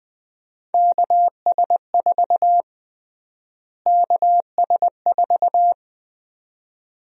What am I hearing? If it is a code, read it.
Morse code, heard as a keyed tone, about 20 wpm, 700 Hz. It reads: KS4 KS4
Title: USA Call Sign Prefixes - Repeated 2x - 20wpm